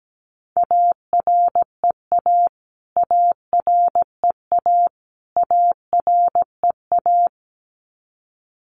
Morse code, heard as a keyed tone, about 17 wpm, 700 Hz. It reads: AREA AREA AREA
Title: Top 500 Words Repeated 17wpm